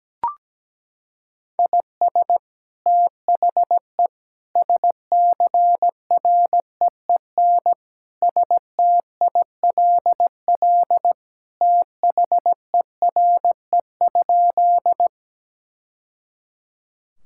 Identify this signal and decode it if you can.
Morse code, heard as a keyed tone, about 17 wpm, 700 Hz. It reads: IS THE SCREEN STILL THERE?
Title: Sentences from Top 1000 Words - Review Part 6 - 17wpm